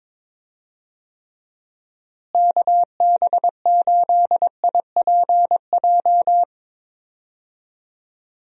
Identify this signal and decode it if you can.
Morse code, heard as a keyed tone, about 22 wpm, 700 Hz. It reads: KB8IPJ
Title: USA Call Signs - 6 Characters - Repeated 3x - 22wpm